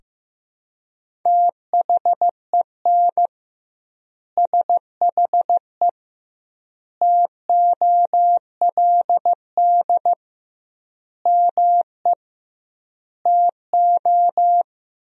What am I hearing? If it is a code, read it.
Morse code, heard as a keyed tone, about 15 wpm, 700 Hz. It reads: THEN SHE TOLD ME TO
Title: 2x Word-Spacing - Sentences from Top 400 Words - 15wpm